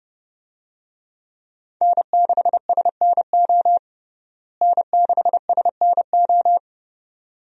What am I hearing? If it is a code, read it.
Morse code, heard as a keyed tone, about 30 wpm, 700 Hz. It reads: N6SNO N6SNO
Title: USA Call Signs - 5 Characters - Repeated 2x - 30wpm